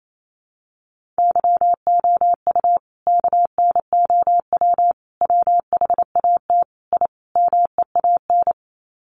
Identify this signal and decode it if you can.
Morse code, heard as a keyed tone, about 28 wpm, 700 Hz. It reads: YOU KNOW WHAT I MEAN
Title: Sets of 5 Words 28wpm